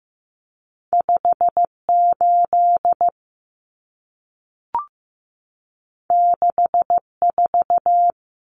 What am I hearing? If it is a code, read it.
Morse code, heard as a keyed tone, about 15 wpm, 700 Hz. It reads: 58 64
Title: Common Cut Numbers - 2-Digit 15wpm